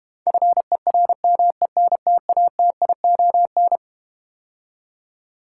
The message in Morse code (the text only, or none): FERMENTATION